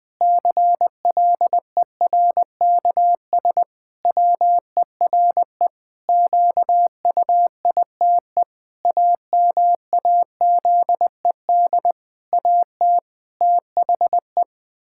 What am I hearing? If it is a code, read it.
Morse code, heard as a keyed tone, about 20 wpm, 700 Hz. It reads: CLERKS WERE QUITE AMAZED AT THE
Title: Pangram Challenge v2 - 20wpm